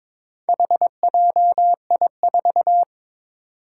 Morse code, heard as a keyed tone, about 23 wpm, 700 Hz. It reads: HJI4
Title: Four Challenging Letter-Number Combinations 22wpm